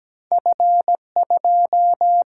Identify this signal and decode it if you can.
Morse code, heard as a keyed tone, about 17 wpm, 700 Hz. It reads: F2
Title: Single-Letter Single-Number 17wpm